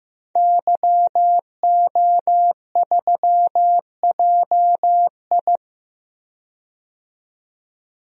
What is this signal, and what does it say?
Morse code, heard as a keyed tone, about 15 wpm, 700 Hz. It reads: YO3JI
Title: DXCC-100 Call Signs - 5 Characters - 15wpm